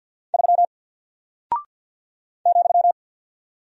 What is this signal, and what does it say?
Morse code, heard as a keyed tone, about 48 wpm, 700 Hz. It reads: 3 ,
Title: Single Character (Letter:Number:Punctuation:Prosign) Alt 50wpm